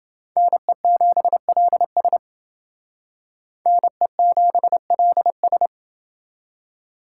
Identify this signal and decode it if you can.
Morse code, heard as a keyed tone, about 29 wpm, 700 Hz. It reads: NE7LS NE7LS